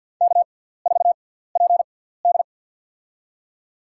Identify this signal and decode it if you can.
Morse code, heard as a keyed tone, about 51 wpm, spaced as for 17 wpm, 690 Hz. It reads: K4PD
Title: USA Call Signs - 4 Characters - 6x Character Spacing - 50wpm